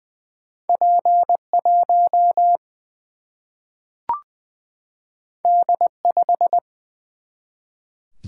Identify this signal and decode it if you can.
Morse code, heard as a keyed tone, about 20 wpm, 700 Hz. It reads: P 1 D 5
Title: Single-Letter Single-Number 20wpm